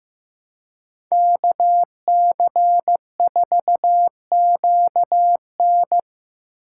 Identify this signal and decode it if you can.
Morse code, heard as a keyed tone, about 18 wpm, 700 Hz. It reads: KC4QN
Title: Speed Racing - USA Call Signs 15wpm